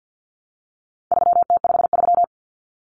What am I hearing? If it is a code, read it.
Morse code, heard as a keyed tone, about 50 wpm, 700 Hz. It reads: 3T53
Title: Common Cut Numbers - 4-Digit 50wpm